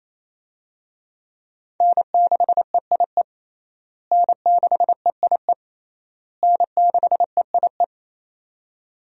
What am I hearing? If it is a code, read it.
Morse code, heard as a keyed tone, about 28 wpm, 700 Hz. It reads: N6EIE N6EIE N6EIE
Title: USA Call Signs - 5 Characters - Repeated 3x - 28wpm